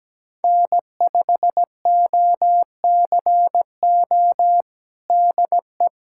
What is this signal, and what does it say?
Morse code, heard as a keyed tone, about 17 wpm, 700 Hz. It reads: N5OCO DE
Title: USA Callsign DE Callsign 17wpm